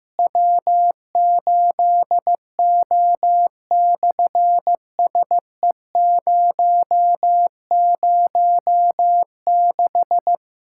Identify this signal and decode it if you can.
Morse code, heard as a keyed tone, about 15 wpm, 700 Hz. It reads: W8O/SE006
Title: SOTA QSO Practice 15wpm